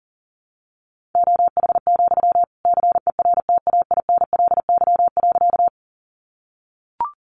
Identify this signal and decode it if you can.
Morse code, heard as a keyed tone, about 40 wpm, 700 Hz. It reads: OH, CERTAINLY.
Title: Sentences from Top 900 Words 40wpm